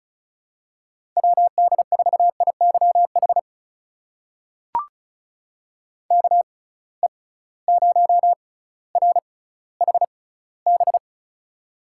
Morse code, heard as a keyed tone, about 36 wpm, spaced as for 18 wpm, 700 Hz. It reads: WD4IYH KE0RHB
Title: USA Call Signs - 6 Characters - 6x Character Spacing - 35wpm